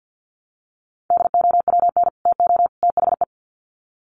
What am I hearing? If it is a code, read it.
Morse code, heard as a keyed tone, about 50 wpm, 700 Hz. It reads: DOWN TO THE
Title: Sets of 3 Words 50wpm